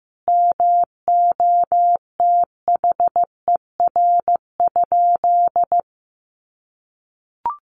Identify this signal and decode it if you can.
Morse code, heard as a keyed tone, about 15 wpm, 700 Hz. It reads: MOTHER?
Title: Sentences from Top 1500 Words 15wpm